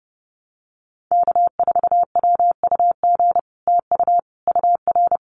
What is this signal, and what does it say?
Morse code, heard as a keyed tone, about 30 wpm, 700 Hz. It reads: K4WUG TU UR
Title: Common QSO Phrases Encore 30wpm